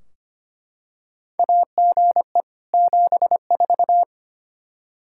Morse code, heard as a keyed tone, about 25 wpm, 700 Hz. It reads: AGE 74
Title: Practice QSO - Exchange 3 - More Info - 25wpm